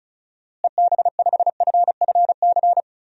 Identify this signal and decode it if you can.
Morse code, heard as a keyed tone, about 36 wpm, 700 Hz. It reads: EB5FFC
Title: DXCC-100 Call Signs - 6 Characters - Repeated 2x - 35wpm